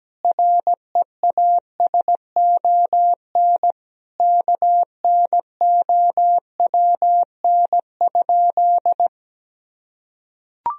Morse code, heard as a keyed tone, about 17 wpm, 700 Hz. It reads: REASON KNOWN?
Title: Sentences from Top 1000 Words - Review Part 3 - 17wpm